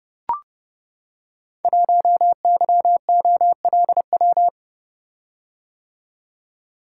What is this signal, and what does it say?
Morse code, heard as a keyed tone, about 30 wpm, 700 Hz. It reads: 1YOLW
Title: Code Groups - 5 Letter:Numbers - 30wpm